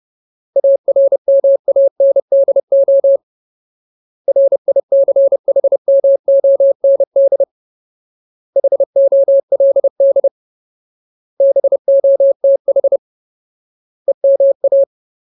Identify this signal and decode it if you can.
Morse code, heard as a keyed tone, about 30 wpm, 550 Hz. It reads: ARMANDO RICHMOND HOLD BOTH EMA
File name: Ultimate Word List / 30WPM Ultimate